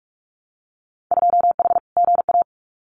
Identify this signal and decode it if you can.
Morse code, heard as a keyed tone, about 46 wpm, 700 Hz. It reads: 2H GA
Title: Winter Field Day Exchanges Encore 45wpm